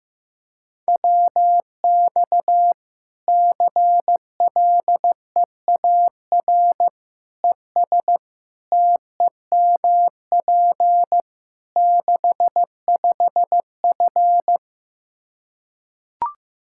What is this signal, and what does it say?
Morse code, heard as a keyed tone, about 15 wpm, 700 Hz. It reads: WX CLEAR ES TEMP 65F
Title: Practice QSO - Exchange 2 - Info - 15wpm